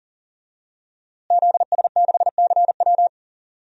Morse code, heard as a keyed tone, about 40 wpm, 700 Hz. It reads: ZS6CW
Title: DXCC-100 Call Signs - 5 Characters - Repeated 2x - 40wpm